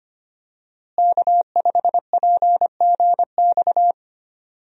Morse code, heard as a keyed tone, about 25 wpm, 700 Hz. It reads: K5PGX
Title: USA Call Signs - 5 Characters - Repeated 2x - 25wpm